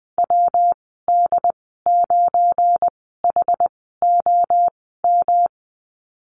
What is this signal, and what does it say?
Morse code, heard as a keyed tone, about 20 wpm, spaced as for 15 wpm, 700 Hz. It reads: WD9HOM
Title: 179 Lesson 29g - Slash - Call Signs 20wpm